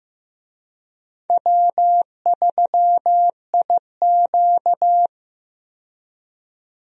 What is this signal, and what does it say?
Morse code, heard as a keyed tone, about 16 wpm, 700 Hz. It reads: W3IQ
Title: USA Call Signs - 4 Characters - 15wpm